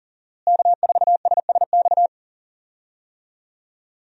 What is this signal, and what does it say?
Morse code, heard as a keyed tone, about 40 wpm, 700 Hz. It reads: K4SSX
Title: USA Call Signs - 5 Characters - Repeated 3x - 40wpm